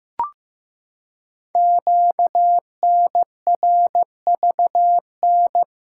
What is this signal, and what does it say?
Morse code, heard as a keyed tone, about 15 wpm, 700 Hz. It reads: QNRVN
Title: Code Groups - 5 Letters - 15wpm